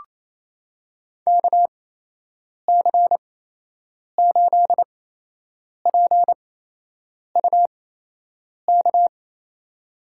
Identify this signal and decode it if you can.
Morse code, heard as a keyed tone, about 28 wpm, spaced as for 8 wpm, 700 Hz. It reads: KC8PUK
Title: USA Call Signs - 6 Characters - 8x Character Spacing - 28wpm